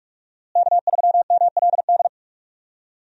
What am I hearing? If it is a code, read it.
Morse code, heard as a keyed tone, about 46 wpm, 700 Hz. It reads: K3MLD
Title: USA Call Signs - 5 Characters - Repeated 2x - 45wpm